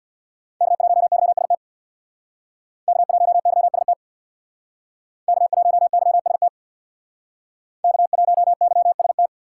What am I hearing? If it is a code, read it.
Morse code, heard as a keyed tone, about 64 wpm, 700 Hz. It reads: K1YST K1YST K1YST K1YST